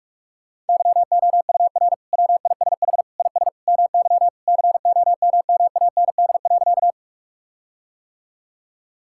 Morse code, heard as a keyed tone, about 46 wpm, 700 Hz. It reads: YOUR WISH IS MY COMMAND.
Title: Sentences from Top 1600 Words 45wpm